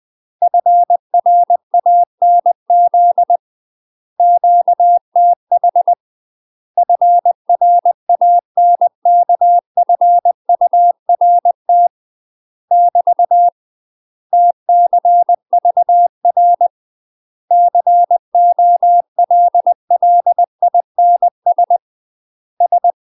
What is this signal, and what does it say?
Morse code, heard as a keyed tone, about 20 wpm, 700 Hz. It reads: FRANZ QTH FRANKFURT <BT> TCVR COLLINS S
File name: UK MASTER QSOs TRAINING / 20WPM / UK MASTER QSOs 0002